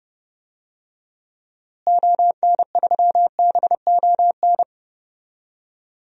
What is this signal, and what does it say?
Morse code, heard as a keyed tone, about 30 wpm, 700 Hz. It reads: ON3BON
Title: DXCC-100 Call Signs - 6 Characters - Repeated 2x - 30wpm